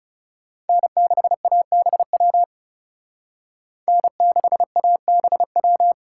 Morse code, heard as a keyed tone, about 38 wpm, 700 Hz. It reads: N6ABW N6ABW